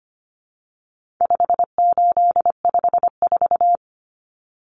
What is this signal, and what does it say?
Morse code, heard as a keyed tone, about 25 wpm, 700 Hz. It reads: 5854
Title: Common Cut Numbers - 4-Digit 25wpm